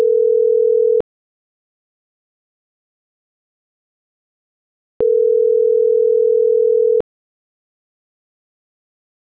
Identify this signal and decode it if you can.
Ringback tone (North America), 440 + 480 Hz.